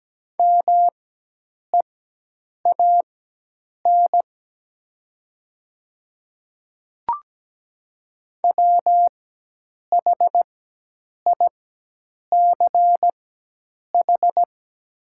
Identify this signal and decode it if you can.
Morse code, heard as a keyed tone, about 17 wpm, spaced as for 8 wpm, 700 Hz. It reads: MEAN WHICH
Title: 4x Character Spacing - Top 500 Words - 17wpm